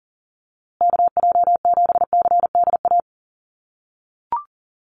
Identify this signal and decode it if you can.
Morse code, heard as a keyed tone, about 40 wpm, 700 Hz. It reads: KJ7CDA
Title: USA Call Signs Repeated 2x 40wpm